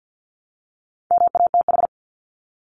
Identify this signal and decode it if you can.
Morse code, heard as a keyed tone, about 50 wpm, 700 Hz. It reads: MATH
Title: Top 751-1000 Words 50wpm